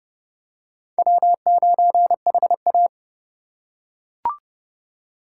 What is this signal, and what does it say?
Morse code, heard as a keyed tone, about 25 wpm, 700 Hz. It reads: W9HA